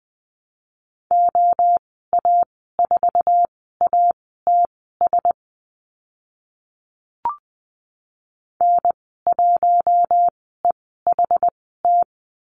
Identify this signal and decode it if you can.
Morse code, heard as a keyed tone, about 20 wpm, spaced as for 15 wpm, 700 Hz. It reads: OA4ATS N1EHT
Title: 076 Lesson 13g - Letter L - Call Signs 20wpm